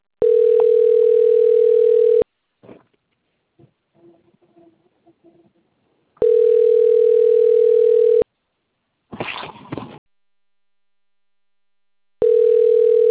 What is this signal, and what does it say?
Ringback tone (North America), 440 + 480 Hz.